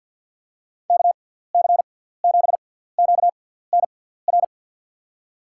Morse code, heard as a keyed tone, about 49 wpm, spaced as for 17 wpm, 700 Hz. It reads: KC7QNR
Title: USA Call Signs - 6 Characters - 6x Character Spacing - 50wpm